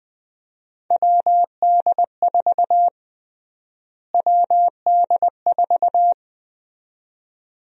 Morse code, heard as a keyed tone, about 20 wpm, 700 Hz. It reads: WD4 WD4
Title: USA Call Sign Prefixes - Repeated 2x - 20wpm